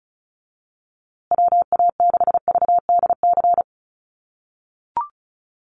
Morse code, heard as a keyed tone, about 34 wpm, 700 Hz. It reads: WA6VDC
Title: USA Call Signs Repeated 2x 35wpm